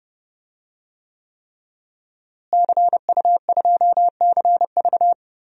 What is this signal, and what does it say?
Morse code, heard as a keyed tone, about 32 wpm, 700 Hz. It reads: CU2CV
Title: DXCC-100 Call Signs - 5 Characters - Repeated 3x - 30wpm